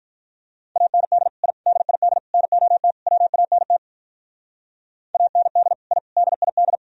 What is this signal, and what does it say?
Morse code, heard as a keyed tone, about 44 wpm, 700 Hz. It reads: AND I DID NOT WANT AND I DID